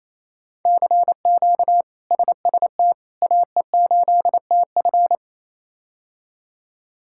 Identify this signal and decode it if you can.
Morse code, heard as a keyed tone, about 28 wpm, 700 Hz. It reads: CQ SST AE8TF
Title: K1USN SST QSO Practice - 28wpm